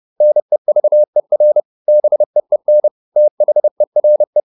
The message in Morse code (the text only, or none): NEVER BEEN THERE